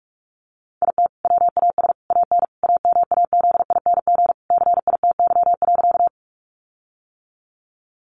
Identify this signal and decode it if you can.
Morse code, heard as a keyed tone, about 44 wpm, 700 Hz. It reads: IT WAS AN AMAZING CITY.
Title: Sentences from Top 1500 Words 45wpm